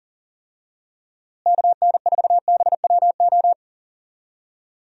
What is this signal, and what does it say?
Morse code, heard as a keyed tone, about 40 wpm, 700 Hz. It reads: KN4BWO
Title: USA Call Signs - 6 Characters - Repeated 2x - 40wpm